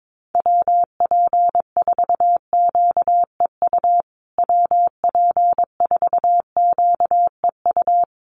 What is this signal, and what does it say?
Morse code, heard as a keyed tone, about 22 wpm, 700 Hz. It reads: WP4QEU WP4QEU
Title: USA Call Signs Repeated 3x 22wpm